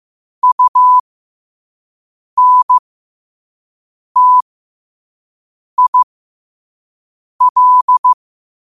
Morse code, heard as a keyed tone, about 15 wpm, spaced as for 5 wpm, 1000 Hz. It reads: UNTIL
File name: qotdfiles / 5wpm